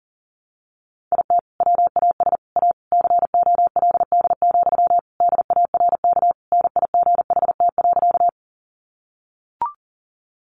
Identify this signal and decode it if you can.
Morse code, heard as a keyed tone, about 40 wpm, 700 Hz. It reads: IT WAS A COLD, DARK NIGHT.
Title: Sentences from Top 500 Words 40wpm